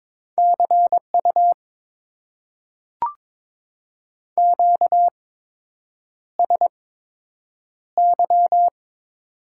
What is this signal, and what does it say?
Morse code, heard as a keyed tone, about 22 wpm, spaced as for 8 wpm, 700 Hz. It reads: CU QSY
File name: Top 90 QSO Elements - 8x Character Spacing - 22wpm